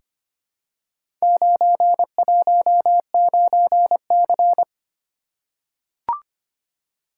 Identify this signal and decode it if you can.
Morse code, heard as a keyed tone, about 25 wpm, 700 Hz. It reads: 919C
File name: practice / 25wpm / Four Letter-Number Combinations - 25wpm